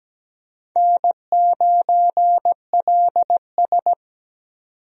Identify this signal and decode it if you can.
Morse code, heard as a keyed tone, about 17 wpm, 700 Hz. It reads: N9LS
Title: USA Call Signs - 4 Characters - 17wpm